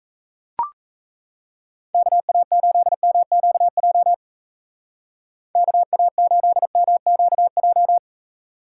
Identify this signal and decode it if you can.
Morse code, heard as a keyed tone, about 36 wpm, 700 Hz. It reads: KA8MQJ KA8MQJ